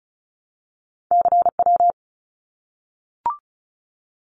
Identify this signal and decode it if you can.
Morse code, heard as a keyed tone, about 34 wpm, 700 Hz. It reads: CW
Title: Abbreviations Repeated 3x - 35wpm